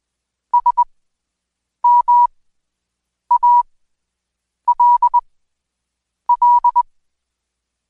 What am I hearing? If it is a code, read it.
Morse code, heard as a keyed tone, about 21 wpm, spaced as for 7 wpm, 970 Hz. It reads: SMALL
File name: Code Quick / AudioDownld2 / Seg113-Day18-DayDrill 18